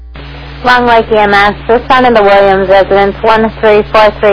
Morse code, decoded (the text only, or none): none